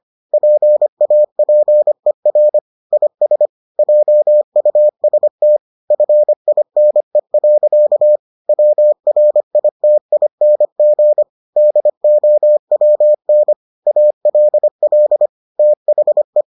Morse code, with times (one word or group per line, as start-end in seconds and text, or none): 0.3-2.6 PAPER
2.9-3.5 IS
3.8-5.6 JUST
5.9-8.2 FINE.
8.5-11.2 WRITING
11.6-13.5 DOWN
13.9-15.3 ALL
15.6-16.4 THE